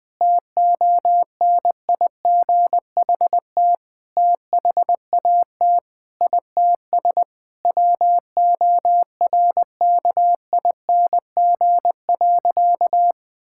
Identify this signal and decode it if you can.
Morse code, heard as a keyed tone, about 20 wpm, 700 Hz. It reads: TONIGHT THAT ITS WORKING.
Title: Sentences from Top 1000 Words - Review Part 6 - 20wpm